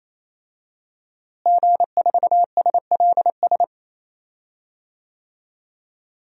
Morse code, heard as a keyed tone, about 28 wpm, 700 Hz. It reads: G4SLS
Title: DXCC-100 Call Signs - 5 Characters - Repeated 3x - 28wpm